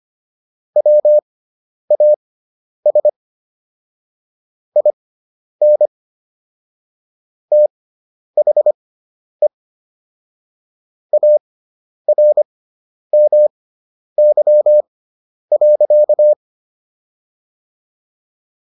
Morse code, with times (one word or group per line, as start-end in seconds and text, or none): 0.8-3.1 WAS
4.8-5.9 IN
7.5-9.5 THE
11.1-16.3 ARMY.